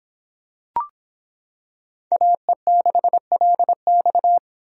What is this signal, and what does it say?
Morse code, heard as a keyed tone, about 26 wpm, 700 Hz. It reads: AE6LX